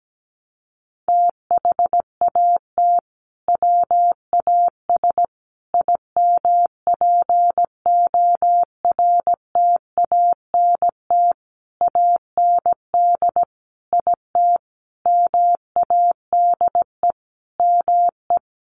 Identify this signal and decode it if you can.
Morse code, heard as a keyed tone, about 17 wpm, 700 Hz. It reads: THAT WAS IMPORTANT AND IT MADE ME
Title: Sentences from Top 600 Words 17wpm